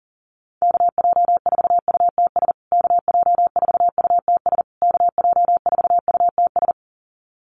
Morse code, heard as a keyed tone, about 40 wpm, 700 Hz. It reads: KJ4UTS KJ4UTS KJ4UTS
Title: USA Call Signs Repeated 3x 40wpm